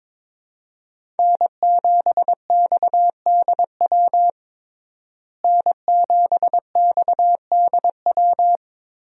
Morse code, heard as a keyed tone, about 22 wpm, 700 Hz. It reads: N7XDW N7XDW
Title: USA Call Signs - 5 Characters - Repeated 3x - 22wpm